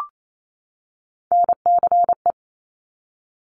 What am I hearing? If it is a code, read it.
Morse code, heard as a keyed tone, about 28 wpm, 700 Hz. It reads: NCE
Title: Top Three Letter Combinations in Words 28wpm